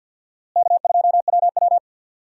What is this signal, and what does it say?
Morse code, heard as a keyed tone, about 52 wpm, 700 Hz. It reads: K2WW
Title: USA Call Signs - 4 Characters - Repeated 2x - 50wpm